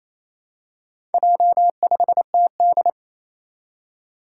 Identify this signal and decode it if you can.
Morse code, heard as a keyed tone, about 27 wpm, 700 Hz. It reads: J5TD
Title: Four Challenging Letter-Number Combinations 28wpm